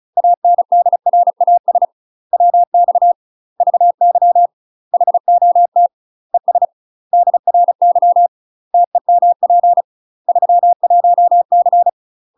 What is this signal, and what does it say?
Morse code, heard as a keyed tone, about 35 wpm, 700 Hz. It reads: ANDRAS WX VY HOT ES DRY TEMP 31C
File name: UK MASTER QSOs TRAINING/35WPM